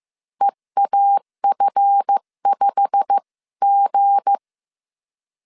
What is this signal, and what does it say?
Morse code, heard as a keyed tone, about 15 wpm, 800 Hz. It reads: EAF5G
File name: CW / DET / 1 9 1500C S B 14wpm 3302